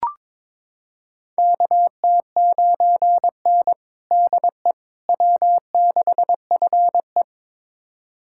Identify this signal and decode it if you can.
Morse code, heard as a keyed tone, about 22 wpm, 700 Hz. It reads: KT9N DE W6FE
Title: USA Callsign DE Callsign - 4 Characters - 22wpm